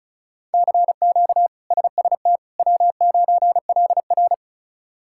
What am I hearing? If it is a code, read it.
Morse code, heard as a keyed tone, about 36 wpm, 700 Hz. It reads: CQ SST W9LR
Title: K1USN SST QSO Practice - 35wpm